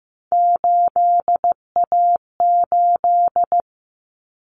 Morse code, heard as a keyed tone, about 15 wpm, 700 Hz. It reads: 8A8
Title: Common Cut Numbers - 3-Digit 15wpm